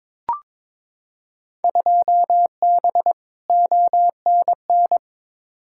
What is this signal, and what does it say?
Morse code, heard as a keyed tone, about 22 wpm, 700 Hz. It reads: 2B ONN
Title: Field Day Exchanges Encore - 22wpm